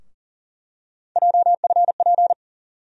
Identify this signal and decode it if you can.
Morse code, heard as a keyed tone, about 40 wpm, 700 Hz. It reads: JFP
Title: USA Call Sign Postfixes - 3 Letters 40wpm